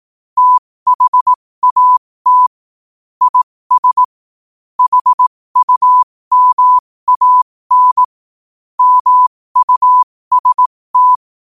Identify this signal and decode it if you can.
Morse code, heard as a keyed tone, about 18 wpm, spaced as for 15 wpm, 1000 Hz. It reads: THAT IS HUMAN MUST
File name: qotdfiles / 15wpm